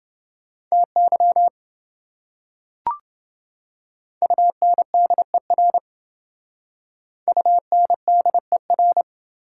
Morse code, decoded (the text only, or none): TY UNDER UNDER